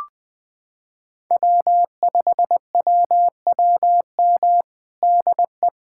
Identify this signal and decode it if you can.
Morse code, heard as a keyed tone, about 20 wpm, 700 Hz. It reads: W5WWM DE
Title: USA Callsign DE Callsign - 5 Characters - 20wpm